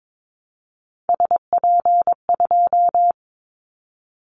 Morse code, heard as a keyed tone, about 22 wpm, 700 Hz. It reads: SP2
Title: Callsign Prefixes for Top 25 DXCC - Repeated 3x - 22wpm